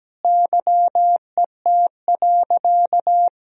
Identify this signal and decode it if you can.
Morse code, heard as a keyed tone, about 17 wpm, 700 Hz. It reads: YET.
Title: Sentences from Top 1000 Words - Review Part 3 - 17wpm